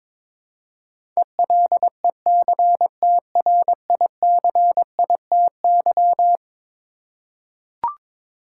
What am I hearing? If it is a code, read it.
Morse code, heard as a keyed tone, about 22 wpm, 700 Hz. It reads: ELECTRICITY
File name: practice / 22wpm / Eleven Letter Words - 22wpm